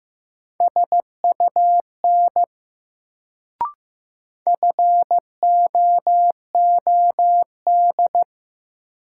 Morse code, heard as a keyed tone, about 15 wpm, 700 Hz. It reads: SUN FOOD
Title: Sentences from Top 300 Words - New Word Drill - 15wpm